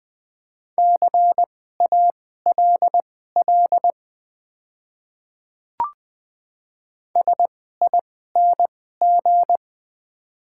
Morse code, heard as a keyed tone, about 20 wpm, spaced as for 14 wpm, 690 Hz. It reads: CALL SING
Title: 2x Character Spacing - Top 500 Words - 20wpm